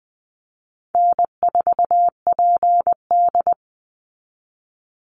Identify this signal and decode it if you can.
Morse code, heard as a keyed tone, about 20 wpm, 700 Hz. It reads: N4PD
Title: Easy Call Signs Repeated 3x - 20wpm